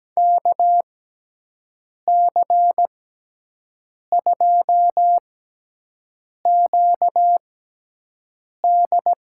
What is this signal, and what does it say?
Morse code, heard as a keyed tone, about 17 wpm, spaced as for 6 wpm, 700 Hz. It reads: KC2QD
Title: USA Call Signs - 6 Characters - 6x Character Spacing - 17wpm